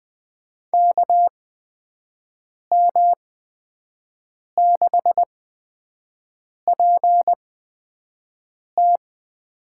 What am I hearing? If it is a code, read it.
Morse code, heard as a keyed tone, about 21 wpm, spaced as for 5 wpm, 700 Hz. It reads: KM6PT
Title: USA Call Signs - 6 Characters - 8x Character Spacing - 20wpm